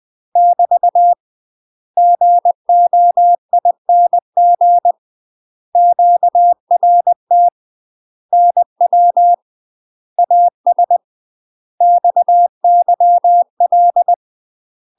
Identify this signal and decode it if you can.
Morse code, heard as a keyed tone, about 20 wpm, 700 Hz. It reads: <BT> GOING QRT NW AS XYL
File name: UK MASTER QSOs TRAINING / 20WPM / UK MASTER QSOs 0067